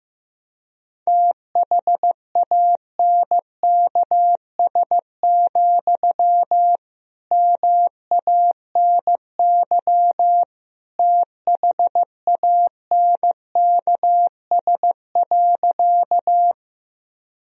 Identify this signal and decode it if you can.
Morse code, heard as a keyed tone, about 15 wpm, 700 Hz. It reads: THANKS, MANY THANKS.
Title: Sentences from Top 1000 Words - Review Part 3 - 15wpm